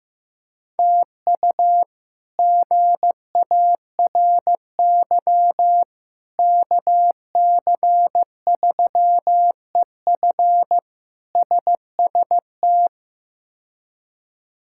Morse code, heard as a keyed tone, about 15 wpm, 700 Hz. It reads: TU GARY KC3EF SST
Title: K1USN SST QSO Practice - 15wpm